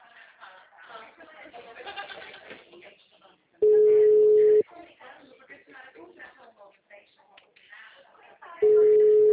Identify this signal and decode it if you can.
Ringback tone (Europe, 425 Hz), with speech.